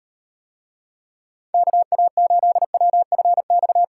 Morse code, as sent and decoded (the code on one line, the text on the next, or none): -.- .- ---.. .-- ..-. -..-
KA8WFX